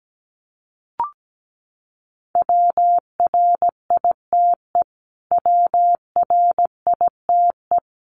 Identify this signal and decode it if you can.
Morse code, heard as a keyed tone, about 17 wpm, 700 Hz. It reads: WRITE WRITE
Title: Top 500 Words Repeated 17wpm